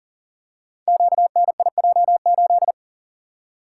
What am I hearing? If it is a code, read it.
Morse code, heard as a keyed tone, about 38 wpm, 700 Hz. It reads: QNIJ8
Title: Code Groups - 5 Letter:Numbers - 40wpm